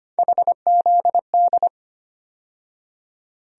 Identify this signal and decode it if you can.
Morse code, heard as a keyed tone, about 25 wpm, 700 Hz. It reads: HZD
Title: USA Call Sign Postfixes - 3 Letters 25wpm